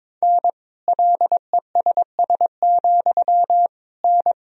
Morse code, heard as a keyed tone, about 22 wpm, 700 Hz. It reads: N LESS, N
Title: Sentences from Top 400 Words and Abbreviations 22wpm